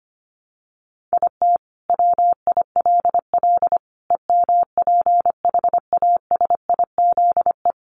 Morse code, heard as a keyed tone, about 25 wpm, 700 Hz. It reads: IT WILL EMPHASIZE
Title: Sentences from Top 1500 Words 25wpm